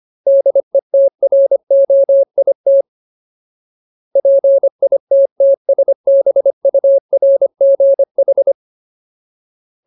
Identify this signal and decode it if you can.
Morse code, heard as a keyed tone, about 25 wpm, 550 Hz. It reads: DETROIT PITTSBURGH
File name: Ultimate Word List / 25WPM Ultimate